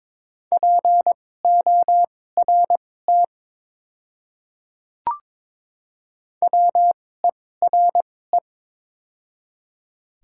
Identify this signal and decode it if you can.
Morse code, heard as a keyed tone, about 22 wpm, spaced as for 16 wpm, 700 Hz. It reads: PORT WERE
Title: 2x Character Spacing - Top 300 Words - 22wpm